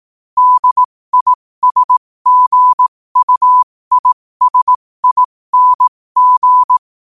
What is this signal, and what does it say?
Morse code, heard as a keyed tone, about 18 wpm, spaced as for 15 wpm, 1000 Hz. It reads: DISGUISING
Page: New quotes every day in morse code at 15 Words per minute.